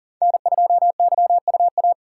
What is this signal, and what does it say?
Morse code, heard as a keyed tone, about 40 wpm, 700 Hz. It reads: N2YUA
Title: USA Call Signs - 5 Characters - 4x Character Spacing - 40wpm